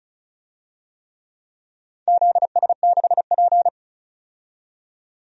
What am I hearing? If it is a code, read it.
Morse code, heard as a keyed tone, about 34 wpm, 700 Hz. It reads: ZS6P
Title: DXCC-100 Call Signs - 4 Characters - 35wpm